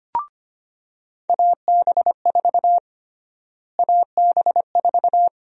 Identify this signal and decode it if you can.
Morse code, heard as a keyed tone, about 25 wpm, 700 Hz. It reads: AB4 AB4
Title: USA Call Sign Prefixes - Repeated 2x - 25wpm